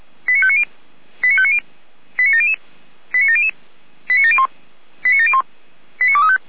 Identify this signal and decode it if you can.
Five-tone selective call (ZVEI), heard as quick runs of five tones, tones of about 70 ms. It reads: 89590 89590 89700 89700 89701 89701 89235